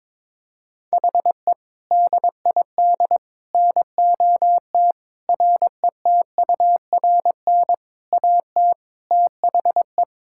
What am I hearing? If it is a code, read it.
Morse code, heard as a keyed tone, about 22 wpm, 700 Hz. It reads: HE DID NOT RETURN AT THE
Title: Sentences from Top 1600 Words 22wpm